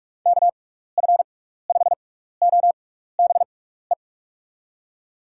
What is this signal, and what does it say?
Morse code, heard as a keyed tone, about 44 wpm, spaced as for 15 wpm, 700 Hz. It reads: KF5OBE
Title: USA Call Signs - 6x Character Spacing - 45wpm